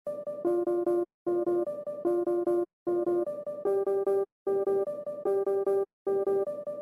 There is no code.